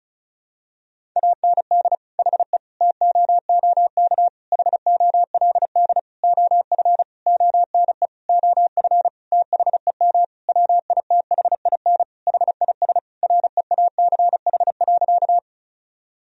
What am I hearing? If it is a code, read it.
Morse code, heard as a keyed tone, about 36 wpm, 700 Hz. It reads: AND HE TOOK HOLD OF ONE OF THEM WITHIN HIS REACH.
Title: Sentences from Top 1000 Words - Review Part 4 - 35wpm